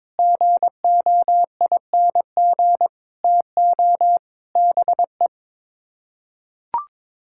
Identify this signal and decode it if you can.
Morse code, heard as a keyed tone, about 22 wpm, 700 Hz. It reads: GOING TO BE
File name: practice / 22wpm / Sets of 7 Words - 22wpm